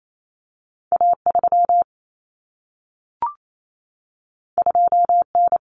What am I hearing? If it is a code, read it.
Morse code, heard as a keyed tone, about 28 wpm, 700 Hz. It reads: A3 2N
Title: Common Cut Numbers - 2-Digit 28wpm